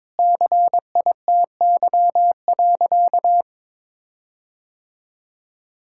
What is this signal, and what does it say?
Morse code, heard as a keyed tone, about 21 wpm, 700 Hz. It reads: CITY.
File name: practice / 22wpm / Sentences with 4-Word Phrases - 22wpm